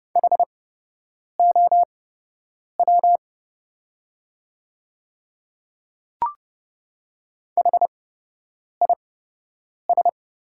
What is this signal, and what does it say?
Morse code, heard as a keyed tone, about 31 wpm, spaced as for 8 wpm, 700 Hz. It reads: HOW HIS
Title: 8x Character Spacing - Top 100 Words - 30wpm